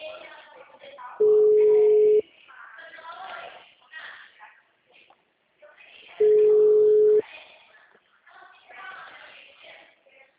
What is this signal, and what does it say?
Ringback tone (Europe, 425 Hz), with speech.
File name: out- -2001- - - .14